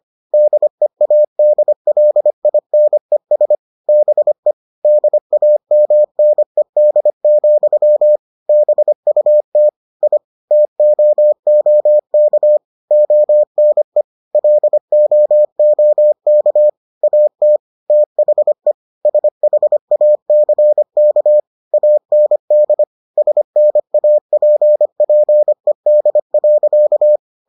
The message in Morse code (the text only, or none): DEADLINES BE DAMNED, BUT I TOOK ONE LOOK AT THE SHACK AND SNAPPED.